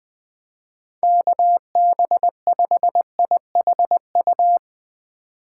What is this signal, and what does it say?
Morse code, heard as a keyed tone, about 20 wpm, 700 Hz. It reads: KB5IHU